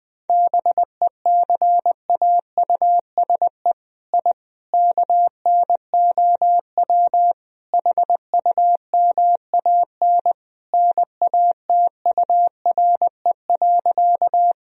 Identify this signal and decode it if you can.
Morse code, heard as a keyed tone, about 21 wpm, 700 Hz. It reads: BECAUSE I KNOW HUMAN NATURE.
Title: Sentences from Top 1000 Words - Review Part 3 - 20wpm